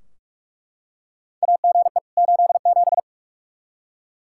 Morse code, heard as a keyed tone, about 42 wpm, 700 Hz. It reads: AGE 87
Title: K1USN SST QSO Practice 45wpm